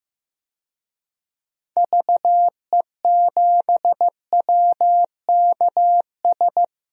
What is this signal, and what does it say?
Morse code, heard as a keyed tone, about 15 wpm, 700 Hz. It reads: VE7WKS
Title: DXCC-100 Call Signs - 6 Characters - Repeated 3x - 15wpm